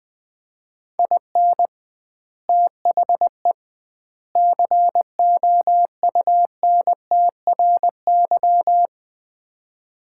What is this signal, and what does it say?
Morse code, heard as a keyed tone, about 20 wpm, 700 Hz. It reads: IN THE COUNTRY
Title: 2x Word-Spacing - Sets of 3 Words - 20wpm